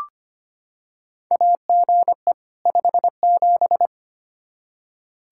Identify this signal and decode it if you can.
Morse code, heard as a keyed tone, about 25 wpm, 700 Hz. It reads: AGE 57
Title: Practice QSO - Exchange 3 - More Info - 25wpm